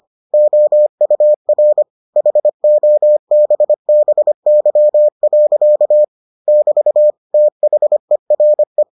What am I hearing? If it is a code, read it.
Morse code, heard as a keyed tone, about 25 wpm, 600 Hz. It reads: OUR HOBBY. <BT> THERE